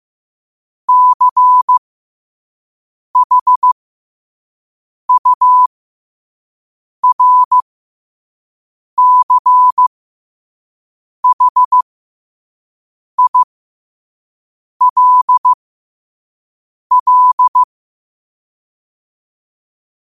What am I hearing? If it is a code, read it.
Morse code, heard as a keyed tone, about 15 wpm, spaced as for 5 wpm, 1000 Hz. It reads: CHURCHILL
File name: qotdfiles / 5wpm